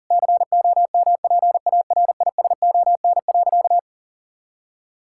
Morse code, heard as a keyed tone, about 37 wpm, 700 Hz. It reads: COMPARISON.
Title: Sentences from Top 1600 Words 40wpm